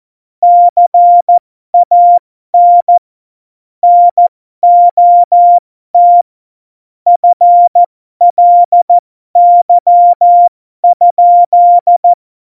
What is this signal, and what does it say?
Morse code, heard as a keyed tone, about 14 wpm, spaced as for 12 wpm, 700 Hz. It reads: CAN NOT FLY?
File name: Morsekurs / Witze